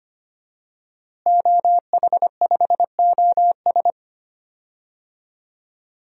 Morse code, heard as a keyed tone, about 25 wpm, 700 Hz. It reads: OH5OS
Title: DXCC-100 Call Signs - 5 Characters - Repeated 2x - 25wpm